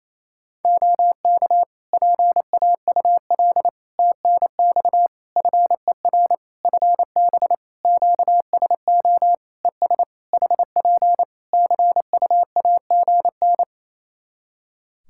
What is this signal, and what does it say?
Morse code, heard as a keyed tone, about 27 wpm, 700 Hz. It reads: OK PAUL TNX FER FB QSO ES HP CUAGN
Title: K1USN SST QSO Practice 28wpm